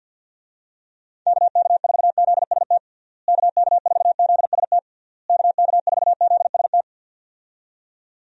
Morse code, heard as a keyed tone, about 48 wpm, 700 Hz. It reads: KK4ZST KK4ZST KK4ZST
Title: USA Call Signs - 6 Characters - Repeated 3x - 50wpm